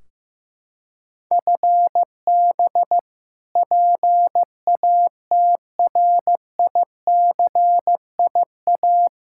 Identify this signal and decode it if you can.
Morse code, heard as a keyed tone, about 15 wpm, 700 Hz. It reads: FB PATRICIA
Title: Practice QSO - Exchange 3 - More Info - 15wpm